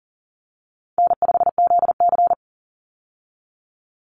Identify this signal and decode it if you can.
Morse code, heard as a keyed tone, about 40 wpm, 700 Hz. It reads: N5ZC
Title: Easy Call Signs Repeated 3x - 40wpm